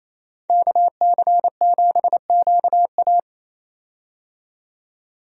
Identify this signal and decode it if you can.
Morse code, heard as a keyed tone, about 28 wpm, 700 Hz. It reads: KC7QA
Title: USA Call Signs - 5 Characters - 28wpm